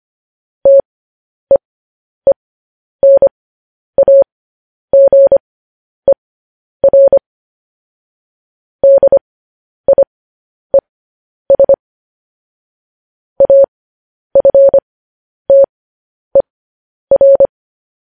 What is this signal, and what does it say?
Morse code, heard as a keyed tone, about 25 wpm, spaced as for 10 wpm, 550 Hz. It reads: TEENAGER DIES AFTER
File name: podcast / News headlines-1x-25x10